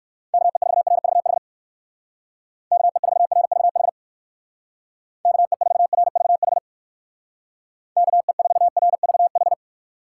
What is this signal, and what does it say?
Morse code, heard as a keyed tone, about 59 wpm, 700 Hz. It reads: KE4RVH KE4RVH KE4RVH KE4RVH